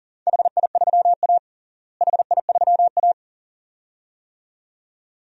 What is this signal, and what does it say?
Morse code, heard as a keyed tone, about 40 wpm, 700 Hz. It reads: HI3A HI3A
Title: DXCC-100 Call Signs - 4 Characters - Repeated 2x - 40wpm